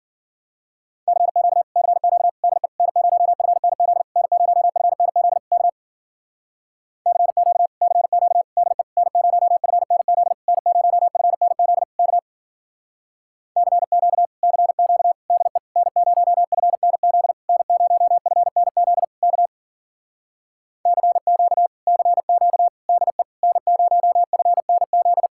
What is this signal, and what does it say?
Morse code, heard as a keyed tone, about 51 wpm, 700 Hz. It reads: CQ CQ DE N0FNZ N0FNZ K CQ CQ DE N0FNZ N0FNZ K CQ CQ DE N0FNZ N0FNZ K CQ CQ DE N0FNZ